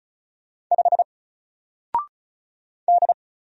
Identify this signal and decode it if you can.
Morse code, heard as a keyed tone, about 35 wpm, 700 Hz. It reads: 5 D
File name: Single-Letter-Number-35wpm